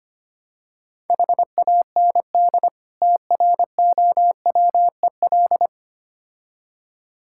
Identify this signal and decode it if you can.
Morse code, heard as a keyed tone, about 27 wpm, 700 Hz. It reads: HAND TROWEL
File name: practice / 25wpm / Difficult Words - Speed-Racing with Hints - 25wpm